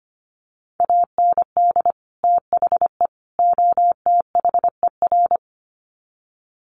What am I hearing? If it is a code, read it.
Morse code, heard as a keyed tone, about 25 wpm, 700 Hz. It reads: AND THE OTHER
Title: Sets of 3 Words 25wpm